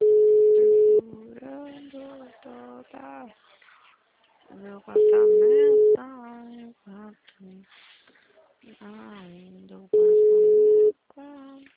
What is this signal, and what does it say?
Ringback tone (Europe, 425 Hz), with speech.